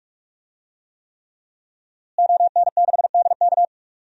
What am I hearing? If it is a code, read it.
Morse code, heard as a keyed tone, about 44 wpm, 700 Hz. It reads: ON6DK
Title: DXCC-100 Call Signs - 5 Characters - Repeated 3x - 45wpm